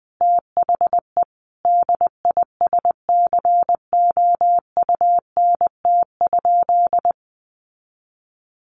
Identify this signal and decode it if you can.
Morse code, heard as a keyed tone, about 20 wpm, 700 Hz. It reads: THE DISCOUNT?
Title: Sentences from Top 1200 Words 20wpm